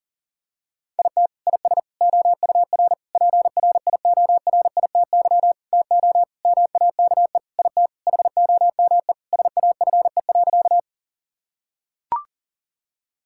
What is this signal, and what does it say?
Morse code, heard as a keyed tone, about 37 wpm, 700 Hz. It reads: IT IS OUR PRIORITY TO MAKE IT HOME SAFE.
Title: Sentences from Top 1600 Words 40wpm